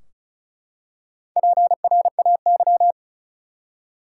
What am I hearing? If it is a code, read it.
Morse code, heard as a keyed tone, about 36 wpm, 700 Hz. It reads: PRAY
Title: Four Letter Words - 35wpm